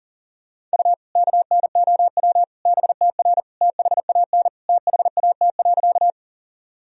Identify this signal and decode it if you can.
Morse code, heard as a keyed tone, about 37 wpm, 700 Hz. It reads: U KNOW BTR THAN THAT.
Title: Sentences from Top 400 Words and Abbreviations 40wpm